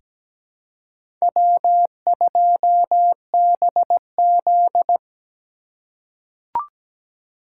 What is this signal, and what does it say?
Morse code, heard as a keyed tone, about 17 wpm, 700 Hz. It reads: W2BZ
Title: USA Call Signs - 4x Character Spacing - 17wpm